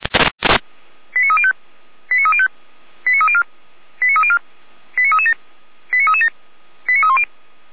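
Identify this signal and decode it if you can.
Five-tone selective call (ZVEI), heard as quick runs of five tones, tones of about 70 ms. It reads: 89305 89305 89304 89304 89337 89337 89310
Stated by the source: SDR-received ZVEI five-tone selective call.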